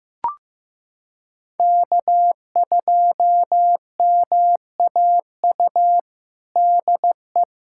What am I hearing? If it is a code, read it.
Morse code, heard as a keyed tone, about 15 wpm, 700 Hz. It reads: K2MAU DE
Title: USA Callsign DE Callsign 15wpm